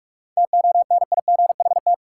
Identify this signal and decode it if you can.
Morse code, heard as a keyed tone, about 46 wpm, 700 Hz. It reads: TONIGHT
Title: Sentences from Top 600 Words - New Word Drill - 45wpm